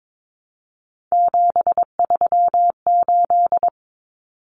Morse code, heard as a keyed tone, about 22 wpm, 700 Hz. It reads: 738
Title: Common Cut Numbers - 3-Digit 22wpm